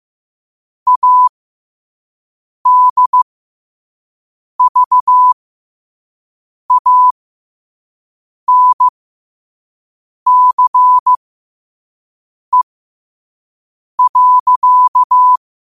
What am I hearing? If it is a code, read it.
Morse code, heard as a keyed tone, about 15 wpm, spaced as for 5 wpm, 1000 Hz. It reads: ADVANCE.
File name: qotdfiles / 5wpm